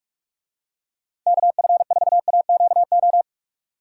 Morse code, heard as a keyed tone, about 45 wpm, 700 Hz. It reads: KF4AQO